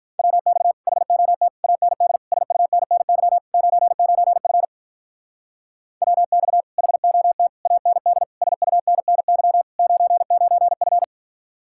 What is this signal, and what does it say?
Morse code, heard as a keyed tone, about 44 wpm, 700 Hz. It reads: WX HOT AND SUNNY 99F WX HOT AND SUNNY 99F